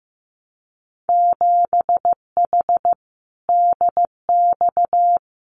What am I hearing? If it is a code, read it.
Morse code, heard as a keyed tone, about 15 wpm, 700 Hz. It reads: 7H DX
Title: Winter Field Day Exchanges Encore 15wpm